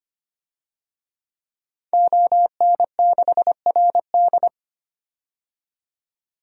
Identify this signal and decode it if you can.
Morse code, heard as a keyed tone, about 25 wpm, 700 Hz. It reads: ON6RD
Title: DXCC-100 Call Signs - 5 Characters - 25wpm